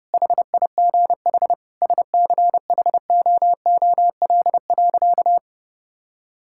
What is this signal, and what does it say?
Morse code, heard as a keyed tone, about 30 wpm, 700 Hz. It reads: HIGH SCHOOL.
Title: Sentences from Top 1000 Words - Review Part 6 - 30wpm